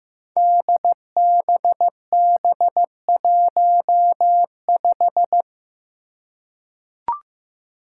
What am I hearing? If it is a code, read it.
Morse code, heard as a keyed tone, about 15 wpm, 700 Hz. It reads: DBB15
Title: Five Challenging Letter-Number Combinations 15wpm